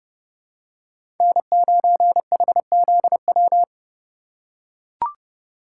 Morse code, heard as a keyed tone, about 30 wpm, 700 Hz. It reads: N9HZW
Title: USA Call Signs - 5 Characters - 8x Character Spacing - 30wpm